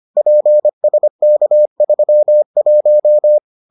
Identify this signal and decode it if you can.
Morse code, heard as a keyed tone, about 25 wpm, 600 Hz. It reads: PSK31